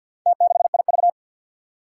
Morse code, heard as a keyed tone, about 47 wpm, 700 Hz. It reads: T6IV
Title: Four Challenging Letter-Number Combinations 50wpm